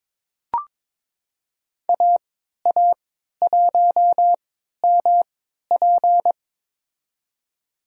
Morse code, heard as a keyed tone, about 22 wpm, spaced as for 12 wpm, 700 Hz. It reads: AA1MP
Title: USA Call Signs - 5 Characters - 3x Character Spacing - 22wpm